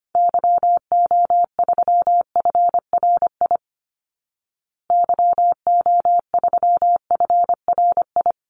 Morse code, heard as a keyed tone, about 25 wpm, 700 Hz. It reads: YO3FRI YO3FRI
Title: Call Signs Repeated 3x - 25wpm